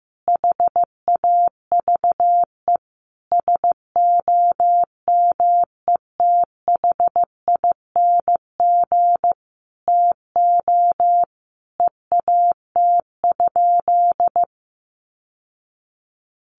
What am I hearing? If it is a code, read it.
Morse code, heard as a keyed tone, about 15 wpm, 700 Hz. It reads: HAVE SOMETHING TO EAT?
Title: Sentences from Top 600 Words 15wpm